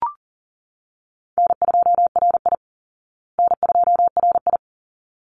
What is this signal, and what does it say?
Morse code, heard as a keyed tone, about 40 wpm, 700 Hz. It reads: N2RI N2RI
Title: Call Signs Repeated 3x - 40wpm